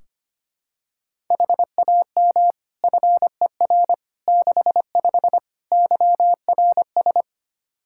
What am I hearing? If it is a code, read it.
Morse code, heard as a keyed tone, about 25 wpm, 700 Hz. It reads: HAM FER 65 YRS
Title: Practice QSO - Exchange 3 - More Info - 25wpm